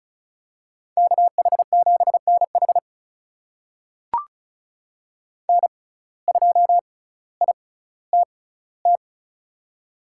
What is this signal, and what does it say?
Morse code, heard as a keyed tone, about 36 wpm, spaced as for 18 wpm, 700 Hz. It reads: KH7NH N2ITT
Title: USA Call Signs - 5 Characters - 6x Character Spacing - 35wpm